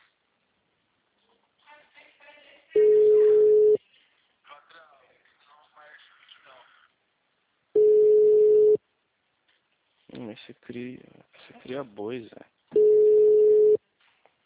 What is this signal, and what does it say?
Ringback tone (Europe, 425 Hz), with speech.